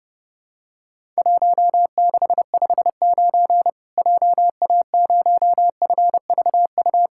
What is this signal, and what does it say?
Morse code, heard as a keyed tone, about 31 wpm, 700 Hz. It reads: 1659 JA0FVU
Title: Worked All Europe DX Contest - QTC Exchange - 30wpm